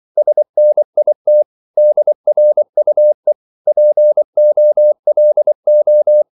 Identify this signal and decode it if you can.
Morse code, heard as a keyed tone, about 24 wpm, 600 Hz. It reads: SNIT DRUE POLO
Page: Korte danske ord 24wpm